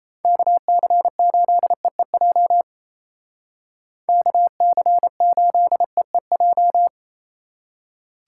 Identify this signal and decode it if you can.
Morse code, heard as a keyed tone, about 36 wpm, 700 Hz. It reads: KC8EEJ KC8EEJ